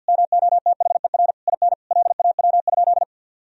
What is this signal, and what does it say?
Morse code, heard as a keyed tone, about 50 wpm, 700 Hz. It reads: MOTHER IN LAW?
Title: Sentences from Top 1000 Words - Review Part 4 - 50wpm